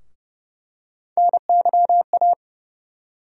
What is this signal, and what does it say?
Morse code, heard as a keyed tone, about 31 wpm, 700 Hz. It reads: NYA
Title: USA Call Sign Postfixes - 3 Letters 30wpm